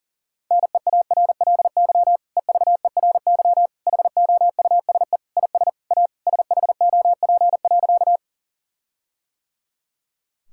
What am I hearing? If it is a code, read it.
Morse code, heard as a keyed tone, about 40 wpm, 700 Hz. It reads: NEARLY EVERY HOUSE IS A SHOP.
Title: Sentences from Top 1600 Words 40wpm